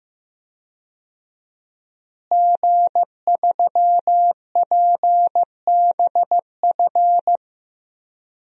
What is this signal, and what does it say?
Morse code, heard as a keyed tone, about 15 wpm, 700 Hz. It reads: G3PBF
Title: DXCC-100 Call Signs - 5 Characters - Repeated 3x - 15wpm